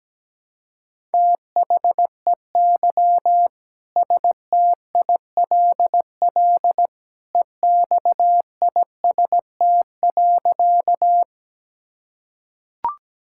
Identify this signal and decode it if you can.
Morse code, heard as a keyed tone, about 17 wpm, 700 Hz. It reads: THEY STILL EXIST.
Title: Sentences from Top 1000 Words - Review Part 5 - 17wpm